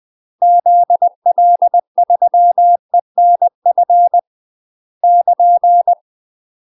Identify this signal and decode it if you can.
Morse code, heard as a keyed tone, about 20 wpm, 700 Hz. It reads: ZL3ENF <KN>
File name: UK MASTER QSOs TRAINING / 20WPM / UK MASTER QSOs 0031